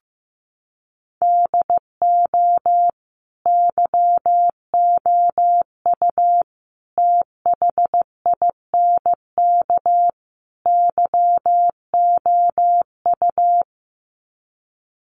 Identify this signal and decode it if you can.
Morse code, heard as a keyed tone, about 15 wpm, 700 Hz. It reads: DO YOU THINK YOU
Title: Sets of 4 Words 15wpm